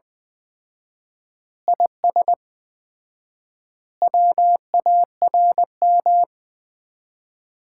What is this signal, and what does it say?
Morse code, heard as a keyed tone, about 20 wpm, 700 Hz. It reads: IS WARM
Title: Common QSO Phrases - 4x Word-Spacing - 20wpm